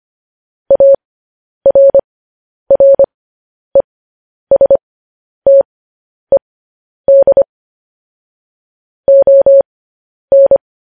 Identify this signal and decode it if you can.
Morse code, heard as a keyed tone, about 25 wpm, spaced as for 10 wpm, 550 Hz. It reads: ARRESTED ON
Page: news-headlines-in-morse-code-at-10-wpm